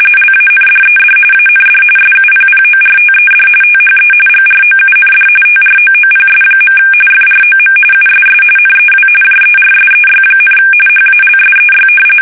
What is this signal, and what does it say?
RTTY, about 45.45 baud. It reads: LANDED AND SUPPORTED THE GROUND TROOPS. FROM USS NEW JERSEY, HALSEY
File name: fsk850 / battleshipnj